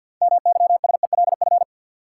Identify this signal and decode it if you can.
Morse code, heard as a keyed tone, about 50 wpm, 700 Hz. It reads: MYSELF
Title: Sentences from Top 1300 Words - New Word Drill Alt - 50wpm